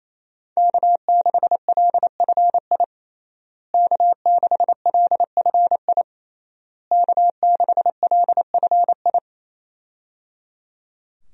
Morse code, heard as a keyed tone, about 27 wpm, 700 Hz. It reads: K6LFI K6LFI K6LFI
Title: USA Call Signs - 5 Characters - Repeated 3x - 28wpm